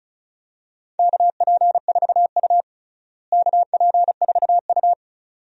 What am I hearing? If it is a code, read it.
Morse code, heard as a keyed tone, about 36 wpm, 700 Hz. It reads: KP4U KP4U
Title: DXCC-100 Call Signs - 4 Characters - Repeated 2x - 35wpm